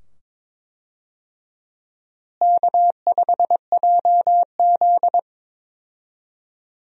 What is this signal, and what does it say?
Morse code, heard as a keyed tone, about 22 wpm, 700 Hz. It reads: K5JZ
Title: USA Call Signs - 4 Characters - Repeated 2x - 22wpm